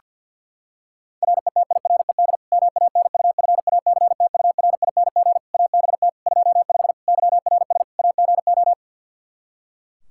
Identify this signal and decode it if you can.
Morse code, heard as a keyed tone, about 52 wpm, 700 Hz. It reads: RETIRED MANUFACTURING ABT 25 YRS AGO
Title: K1USN SST QSO Practice 50wpm